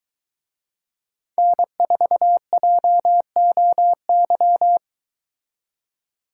Morse code, heard as a keyed tone, about 22 wpm, 700 Hz. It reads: N4JOY